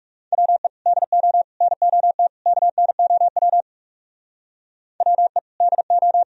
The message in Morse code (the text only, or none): WE DO NOT KNOW WE DO